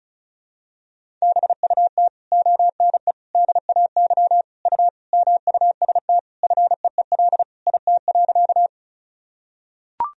Morse code, heard as a keyed tone, about 36 wpm, 700 Hz. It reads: BUT ONE DAY U MUST FEEL IT.
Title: Sentences from Top 400 Words and Abbreviations 35wpm